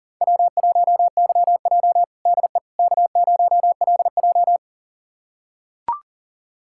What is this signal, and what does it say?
Morse code, heard as a keyed tone, about 37 wpm, 700 Hz. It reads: W1YJ DE K0LJ
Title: USA Callsign DE Callsign - 4 Characters - 40wpm